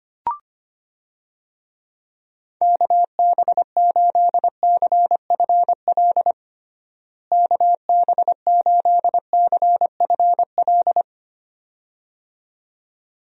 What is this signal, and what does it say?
Morse code, heard as a keyed tone, about 25 wpm, 700 Hz. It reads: KB8CFL KB8CFL
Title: USA Call Signs - 6 Characters - Repeated 2x - 25wpm